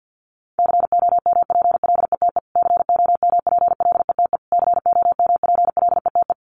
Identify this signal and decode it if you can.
Morse code, heard as a keyed tone, about 50 wpm, 700 Hz. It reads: COMPLETE COMPLETE COMPLETE
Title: Top 500 Words Repeated 50wpm